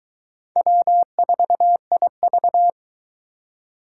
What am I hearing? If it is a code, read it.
Morse code, heard as a keyed tone, about 22 wpm, 700 Hz. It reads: W4IV